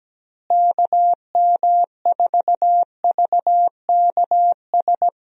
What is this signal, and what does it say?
Morse code, heard as a keyed tone, about 17 wpm, 700 Hz. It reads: KM4VKS
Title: USA Call Signs - 6 Characters - Repeated 2x - 17wpm